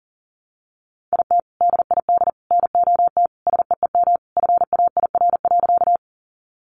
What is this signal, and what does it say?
Morse code, heard as a keyed tone, about 40 wpm, 700 Hz. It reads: IT DID NOT SEEM FAIR.
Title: Sentences from Top 1300 Words 40wpm